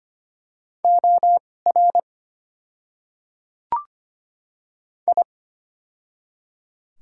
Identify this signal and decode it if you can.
Morse code, heard as a keyed tone, about 25 wpm, spaced as for 18 wpm, 690 Hz. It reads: OR I
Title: 2x Character Spacing - Top 100 Words - 25wpm